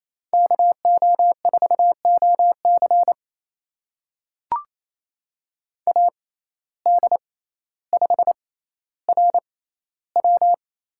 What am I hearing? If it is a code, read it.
Morse code, heard as a keyed tone, about 28 wpm, spaced as for 14 wpm, 700 Hz. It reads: KO4OC AD5RW
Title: USA Call Signs - 6x Character Spacing - 28wpm